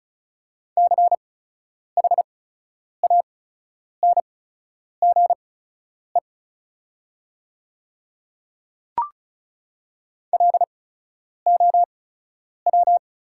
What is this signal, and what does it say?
Morse code, heard as a keyed tone, about 34 wpm, spaced as for 10 wpm, 690 Hz. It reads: CHANGE LOW
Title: 8x Character Spacing - Top 200 Words - 35wpm